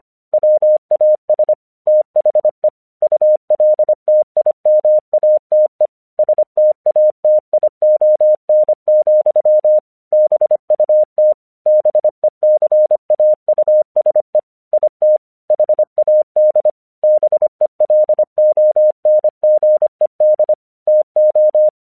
Morse code, heard as a keyed tone, about 25 wpm, 610 Hz. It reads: WAS THE ULTIMATE STATION, BUT BECAUSE IT HAD BELONGED TO
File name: 1 English 25wpm / AQC12